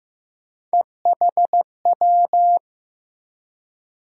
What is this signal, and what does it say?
Morse code, heard as a keyed tone, about 15 wpm, 700 Hz. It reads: EHW
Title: USA Call Sign Postfixes - 3 Letters 15wpm